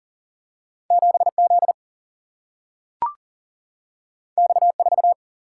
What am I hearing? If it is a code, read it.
Morse code, heard as a keyed tone, about 38 wpm, 700 Hz. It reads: 7 Z X 4
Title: Single-Letter Single-Number 40wpm